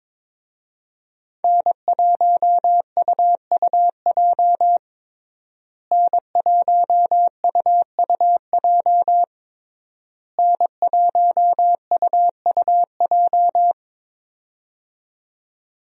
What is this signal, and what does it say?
Morse code, heard as a keyed tone, about 22 wpm, 700 Hz. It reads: N1UUJ N1UUJ N1UUJ
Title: USA Call Signs - 5 Characters - Repeated 3x - 22wpm